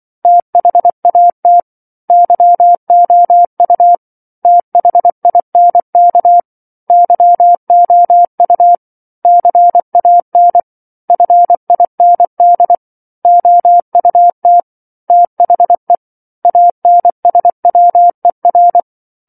Morse code, heard as a keyed tone, about 24 wpm, 700 Hz. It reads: THAT YOU THINK YOU CAN FIND OUT THE ANSWER
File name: 07-Chapter-07 24 WPM